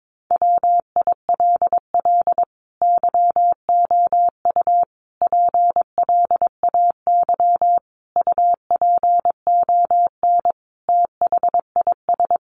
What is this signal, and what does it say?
Morse code, heard as a keyed tone, about 22 wpm, 700 Hz. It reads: WILL YOU PLAY UPON THIS
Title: Sentences from Top 1500 Words 22wpm